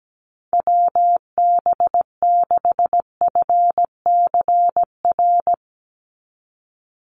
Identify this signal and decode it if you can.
Morse code, heard as a keyed tone, about 17 wpm, 700 Hz. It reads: WB6FCR
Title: USA Call Signs 17wpm